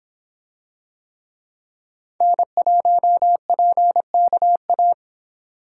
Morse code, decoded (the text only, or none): N1PKA